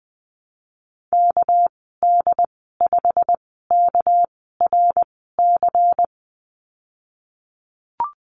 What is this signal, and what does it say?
Morse code, heard as a keyed tone, about 20 wpm, spaced as for 15 wpm, 700 Hz. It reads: KD5KRC
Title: 200 Lesson 32g - Letter k - Call Signs 20wpm